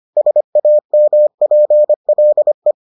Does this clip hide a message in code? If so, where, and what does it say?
Yes: Morse code from 0.2 to 2.7 s, heard as a keyed tone, about 25 wpm, 600 Hz. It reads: SAMPLE